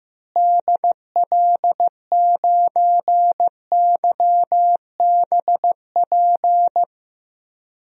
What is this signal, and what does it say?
Morse code, heard as a keyed tone, about 15 wpm, 700 Hz. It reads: DL9YBP
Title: DXCC-100 Call Signs - 6 Characters - Repeated 2x - 15wpm